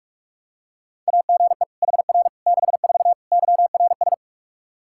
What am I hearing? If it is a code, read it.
Morse code, heard as a keyed tone, about 44 wpm, 700 Hz. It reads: AGE HR 64 YRS
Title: K1USN SST QSO Practice 45wpm